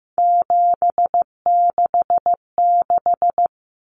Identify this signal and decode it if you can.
Morse code, heard as a keyed tone, about 15 wpm, 700 Hz. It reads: 766
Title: Numbers - 3-Digit 15wpm